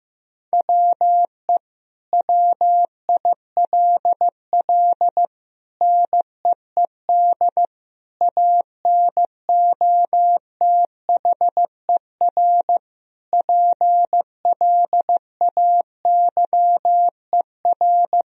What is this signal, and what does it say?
Morse code, heard as a keyed tone, about 15 wpm, 700 Hz. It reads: WE WILL NEED ANOTHER PLAYER
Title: Sentences from Top 1000 Words - Review Part 4 - 15wpm